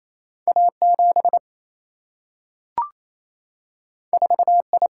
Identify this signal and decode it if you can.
Morse code, heard as a keyed tone, about 28 wpm, 700 Hz. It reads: A 7 4 I
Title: Single-Letter Single-Number 28wpm